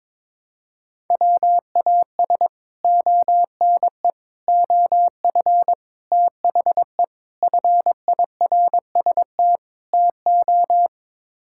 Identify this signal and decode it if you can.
Morse code, heard as a keyed tone, about 23 wpm, 700 Hz. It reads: WAS ONE OF THE FIRST TO
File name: practice / 22wpm / Sets of 6 Words - 22wpm